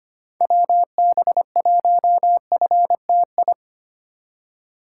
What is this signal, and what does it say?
Morse code, heard as a keyed tone, about 25 wpm, 700 Hz. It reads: WB1FTI
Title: USA Call Signs - 6 Characters - 4x Character Spacing - 25wpm